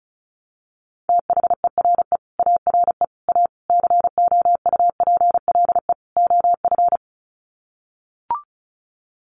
Morse code, heard as a keyed tone, about 34 wpm, 700 Hz. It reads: THERE ARE A COUPLE OF
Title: Sets of 5 Words Encore 35wpm